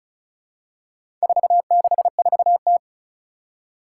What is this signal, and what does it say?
Morse code, heard as a keyed tone, about 36 wpm, 700 Hz. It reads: 464T
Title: Four Challenging Letter-Number Combinations 35wpm